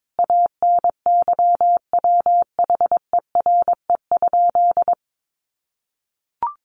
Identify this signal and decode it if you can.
Morse code, heard as a keyed tone, about 22 wpm, 700 Hz. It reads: ANYWHERE?
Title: Sentences from Top 1400 Words 22wpm